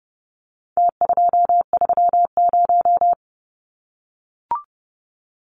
Morse code, heard as a keyed tone, about 30 wpm, 700 Hz. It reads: T230
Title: Numbers - Contest Style - 4-Digit 30wpm